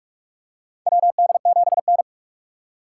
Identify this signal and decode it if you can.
Morse code, heard as a keyed tone, about 41 wpm, 700 Hz. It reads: WD7N
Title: USA Call Signs - 3x Character Spacing - 45wpm